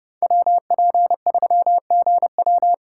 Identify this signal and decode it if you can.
Morse code, heard as a keyed tone, about 30 wpm, 700 Hz. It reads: WP3GW
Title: USA Call Signs - 5 Characters - 6x Character Spacing - 30wpm